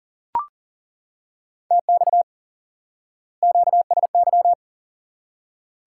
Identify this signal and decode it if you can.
Morse code, heard as a keyed tone, about 40 wpm, 700 Hz. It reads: TX QSY
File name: IQE Course 3 - Lesson 89 - YAGI - Two Elements - 40wpm